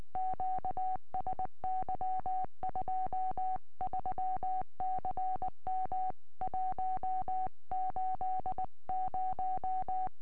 Morse code, heard as a keyed tone, about 19 wpm, 750 Hz. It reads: QSY 23CM 180
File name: tropo / f6dkw 70cm